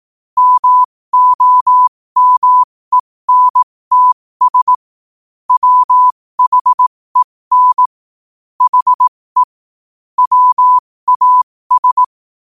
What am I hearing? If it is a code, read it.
Morse code, heard as a keyed tone, about 18 wpm, spaced as for 15 wpm, 1000 Hz. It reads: MOMENTS WHEN HE WAS
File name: qotdfiles / 15wpm